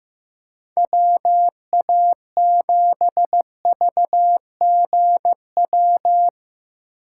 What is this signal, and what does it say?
Morse code, heard as a keyed tone, about 15 wpm, 700 Hz. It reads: WA7VGW
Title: USA Call Signs - 6 Characters - 3x Character Spacing - 15wpm